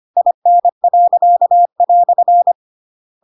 Morse code, heard as a keyed tone, about 25 wpm, 700 Hz. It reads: IN."
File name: Pinocchio / Pinocchio ch7 25 25 ik1hge